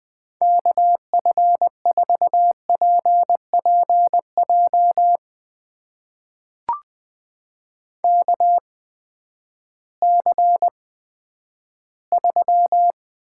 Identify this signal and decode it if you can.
Morse code, heard as a keyed tone, about 20 wpm, spaced as for 11 wpm, 700 Hz. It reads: KF4PPJ KC3
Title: USA Call Signs - 6 Characters - 8x Character Spacing - 20wpm